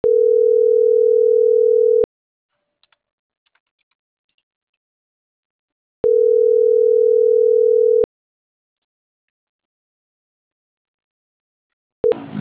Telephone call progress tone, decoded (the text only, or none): ringback tone